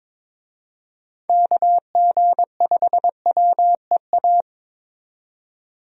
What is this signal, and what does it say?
Morse code, heard as a keyed tone, about 22 wpm, 700 Hz. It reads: KG5WEA
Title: USA Call Signs - 6 Characters - 22wpm